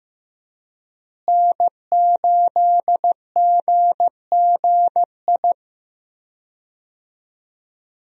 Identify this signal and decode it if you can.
Morse code, heard as a keyed tone, about 16 wpm, 700 Hz. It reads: N8GGI
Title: USA Call Signs - 5 Characters - Repeated 2x - 15wpm